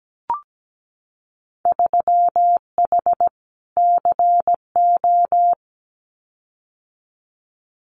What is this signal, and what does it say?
Morse code, heard as a keyed tone, about 17 wpm, 700 Hz. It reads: 3H CO
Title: Winter Field Day Exchanges 17wpm